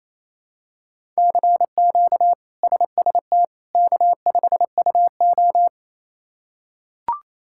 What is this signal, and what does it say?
Morse code, heard as a keyed tone, about 28 wpm, 700 Hz. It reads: CQ SST K5UO
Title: K1USN SST QSO Practice - 28wpm